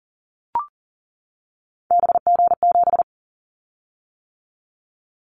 Morse code, heard as a keyed tone, about 40 wpm, 700 Hz. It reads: BG7
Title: Callsign Prefixes for Top 100 DXCC - 40wpm